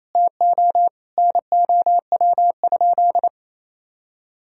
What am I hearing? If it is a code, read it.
Morse code, heard as a keyed tone, about 28 wpm, 700 Hz. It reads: TO NOW?
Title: Sentences from Top 100 Words and Abbreviations 28wpm